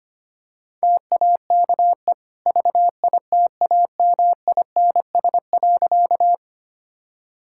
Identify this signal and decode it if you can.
Morse code, heard as a keyed tone, about 25 wpm, 690 Hz. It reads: TAKE VITAMINS.
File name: Easy Sentences with Hints - 25wpm